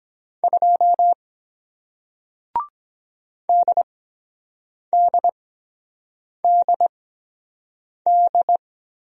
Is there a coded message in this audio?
Yes: Morse code from 0.4 to 8.6 s, heard as a keyed tone, about 23 wpm, spaced as for 7 wpm, 700 Hz. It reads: 2 DDDD